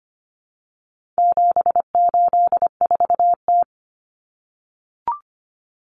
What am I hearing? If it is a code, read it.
Morse code, heard as a keyed tone, about 25 wpm, 700 Hz. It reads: 784T
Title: Common Cut Numbers - 4-Digit 25wpm